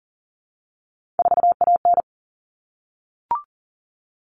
Morse code, heard as a keyed tone, about 40 wpm, 700 Hz. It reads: 4AN
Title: All Cut Numbers - 3-Digit 40wpm